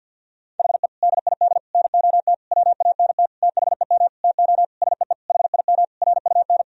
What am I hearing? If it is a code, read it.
Morse code, heard as a keyed tone, about 46 wpm, 700 Hz. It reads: HE DID NOT WANT THEM TO SEE HIM RUN
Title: Sentences from Top 1000 Words - Review Part 1 - 50wpm